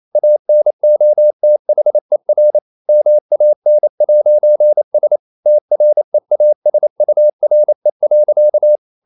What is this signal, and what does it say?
Morse code, heard as a keyed tone, about 28 wpm, 600 Hz. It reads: ANOTHER MAN'S TREASURE.
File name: CW / Practice / EnglishSayings28